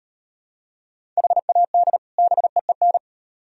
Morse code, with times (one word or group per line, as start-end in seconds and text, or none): 1.2-2.0 HAD
2.2-3.0 BEEN